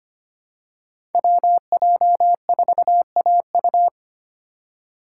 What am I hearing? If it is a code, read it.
Morse code, heard as a keyed tone, about 25 wpm, 700 Hz. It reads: WJ4AU
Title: Code Groups - 5 Letter:Numbers - 25wpm